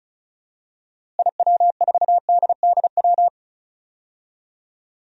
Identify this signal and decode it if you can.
Morse code, heard as a keyed tone, about 34 wpm, 700 Hz. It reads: IW4DDW
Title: DXCC-100 Call Signs - 6 Characters - Repeated 2x - 35wpm